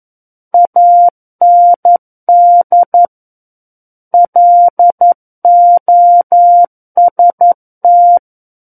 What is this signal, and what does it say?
Morse code, heard as a keyed tone, about 11 wpm, 700 Hz. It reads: AND LOST
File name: 11 WPM